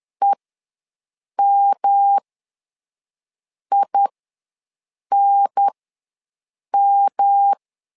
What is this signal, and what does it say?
Morse code, heard as a keyed tone, about 10 wpm, spaced as for 5 wpm, 800 Hz. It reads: EMINM
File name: LEZIONI / 3 MWRO / 3 M ws 10wpm 2121